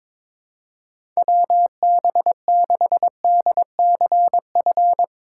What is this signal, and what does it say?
Morse code, heard as a keyed tone, about 21 wpm, 700 Hz. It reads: WB6DCF
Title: USA Call Signs - 6 Characters - 2x Character Spacing - 22wpm